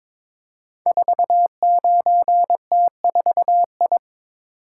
Morse code, heard as a keyed tone, about 22 wpm, 700 Hz. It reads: 49T4I
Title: Code Groups - 5 Letter:Numbers - 22wpm